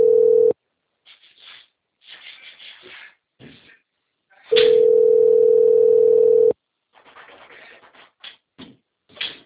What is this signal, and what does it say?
Ringback tone (North America), 440 + 480 Hz.
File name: out- - - - - .6949